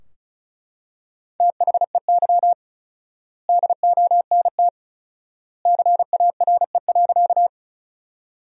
Morse code, heard as a keyed tone, about 34 wpm, 700 Hz. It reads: THEY DONT CARE.
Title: 4x Word-Spacing - Sentences from Top 300 Words - 35wpm